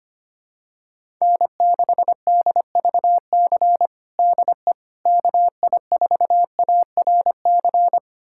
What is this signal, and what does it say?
Morse code, heard as a keyed tone, about 25 wpm, 700 Hz. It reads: N6DVC DE KI4ARC
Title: USA Callsign DE Callsign 25wpm